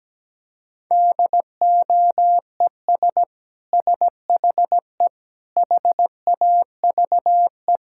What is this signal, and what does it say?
Morse code, heard as a keyed tone, about 17 wpm, 700 Hz. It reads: DOES SHE HAVE
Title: Sentences from Top 1000 Words - Review Part 6 - 17wpm